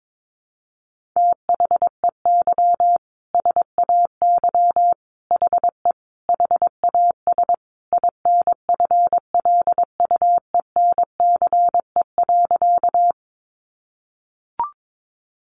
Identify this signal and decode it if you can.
Morse code, heard as a keyed tone, about 22 wpm, 700 Hz. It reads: THEY SAY HE HAS INFLUENCE.
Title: Sentences from Top 1100 Words 22wpm